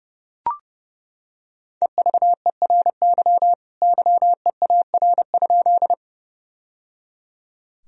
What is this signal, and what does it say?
Morse code, heard as a keyed tone, about 31 wpm, 700 Hz. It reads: EVERY YEAR?
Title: Sentences from Top 1000 Words - Review Part 1 - 30wpm